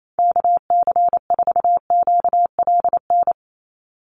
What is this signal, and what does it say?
Morse code, heard as a keyed tone, about 28 wpm, 700 Hz. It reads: KC4QLN
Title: USA Call Signs 28wpm